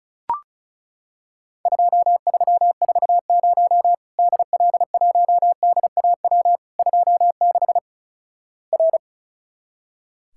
Morse code, heard as a keyed tone, about 36 wpm, 700 Hz. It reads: 2340 DL1DAW 26 R
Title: Worked All Europe DX Contest - QTC Exchange - 35wpm